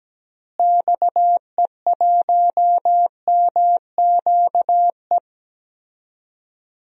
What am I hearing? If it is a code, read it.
Morse code, heard as a keyed tone, about 17 wpm, 700 Hz. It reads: XE1MQE
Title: DXCC-100 Call Signs - 6 Characters - Repeated 3x - 17wpm